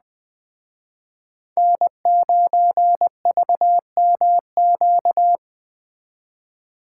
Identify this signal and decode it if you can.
Morse code, heard as a keyed tone, about 20 wpm, 700 Hz. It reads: N9VMQ
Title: Speed Racing - USA 1x3 Call Signs 17wpm